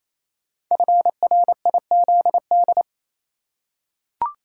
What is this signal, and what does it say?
Morse code, heard as a keyed tone, about 27 wpm, 700 Hz. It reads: FRIZD
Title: Code Groups - 5 Letter:Numbers - 28wpm